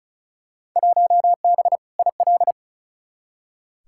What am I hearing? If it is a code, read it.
Morse code, heard as a keyed tone, about 36 wpm, 700 Hz. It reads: 1B IL
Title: Field Day Exchanges Encore - 35wpm